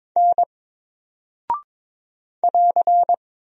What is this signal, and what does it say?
Morse code, heard as a keyed tone, about 22 wpm, 700 Hz. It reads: N <AR>
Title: Single Character (Letter:Number:Punctuation:Prosign) Alt 22wpm